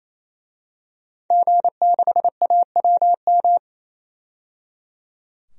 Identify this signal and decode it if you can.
Morse code, heard as a keyed tone, about 28 wpm, 700 Hz. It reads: G6AWM
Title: DXCC-100 Call Signs - 5 Characters - 28wpm